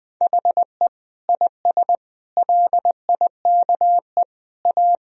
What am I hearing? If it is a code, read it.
Morse code, heard as a keyed tone, about 20 wpm, 700 Hz. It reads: HE IS LIKE A
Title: Sentences from Top 1000 Words - Review Part 1 - 20wpm